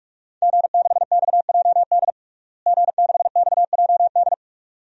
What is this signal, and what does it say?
Morse code, heard as a keyed tone, about 42 wpm, 700 Hz. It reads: G6XJD G6XJD
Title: DXCC-100 Call Signs - 5 Characters - Repeated 2x - 45wpm